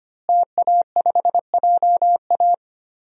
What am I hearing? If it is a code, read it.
Morse code, heard as a keyed tone, about 25 wpm, 700 Hz. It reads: TA5JA
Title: DXCC-100 Call Signs - 5 Characters - Repeated 2x - 25wpm